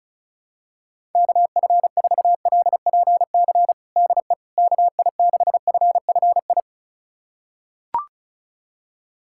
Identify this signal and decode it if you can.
Morse code, heard as a keyed tone, about 34 wpm, 700 Hz. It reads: KF4LPC DE KI6FFI
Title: USA Callsign DE Callsign - 6 Characters - 35wpm